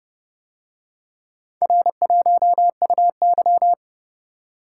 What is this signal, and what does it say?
Morse code, heard as a keyed tone, about 29 wpm, 700 Hz. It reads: R1UY